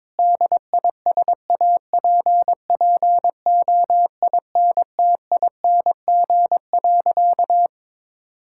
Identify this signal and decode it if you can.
Morse code, heard as a keyed tone, about 22 wpm, 700 Hz. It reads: DISAPPOINTING.
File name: practice / 22wpm / Sentences with 4-Word Phrases - 22wpm